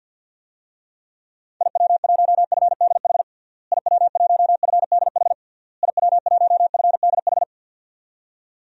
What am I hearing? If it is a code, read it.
Morse code, heard as a keyed tone, about 52 wpm, 700 Hz. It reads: IW1FDH IW1FDH IW1FDH
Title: DXCC-100 Call Signs - 6 Characters - Repeated 3x - 50wpm